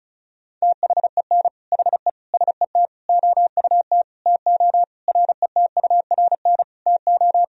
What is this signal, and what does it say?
Morse code, heard as a keyed tone, about 34 wpm, 700 Hz. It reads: THEN HE SET OUT TO RETURN TO
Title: Sentences from Top 1000 Words - Review Part 1 - 35wpm